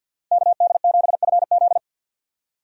Morse code, heard as a keyed tone, about 51 wpm, 700 Hz. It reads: KD7FZ
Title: USA Call Signs - 5 Characters - 50wpm